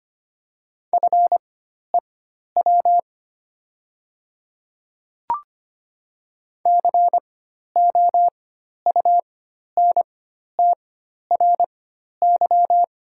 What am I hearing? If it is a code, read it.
Morse code, heard as a keyed tone, about 25 wpm, spaced as for 12 wpm, 700 Hz. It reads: FEW COUNTRY
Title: 4x Character Spacing - Top 300 Words - 25wpm